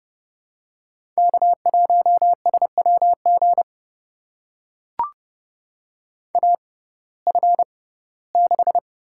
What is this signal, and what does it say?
Morse code, heard as a keyed tone, about 31 wpm, spaced as for 19 wpm, 700 Hz. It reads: K1SWG AF6
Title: USA Call Signs - 5 Characters - 6x Character Spacing - 30wpm